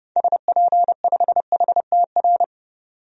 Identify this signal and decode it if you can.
Morse code, heard as a keyed tone, about 30 wpm, 700 Hz. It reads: SP5HTR
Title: DXCC-100 Call Signs - 6 Characters - 30wpm